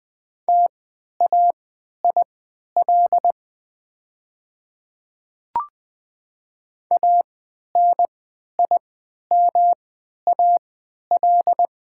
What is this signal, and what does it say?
Morse code, heard as a keyed tone, about 20 wpm, spaced as for 11 wpm, 700 Hz. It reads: TAIL ANIMAL
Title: 3x Character Spacing - Top 500 Words - 20wpm